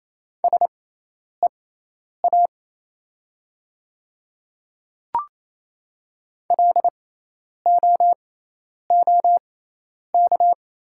Morse code, heard as a keyed tone, about 28 wpm, spaced as for 10 wpm, 700 Hz. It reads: SEA LOOK
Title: 6x Character Spacing - Top 300 Words - 28wpm